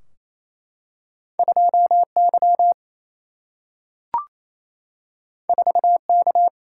Morse code, heard as a keyed tone, about 28 wpm, 700 Hz. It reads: 2Y 4K
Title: Single-Letter Single-Number 28wpm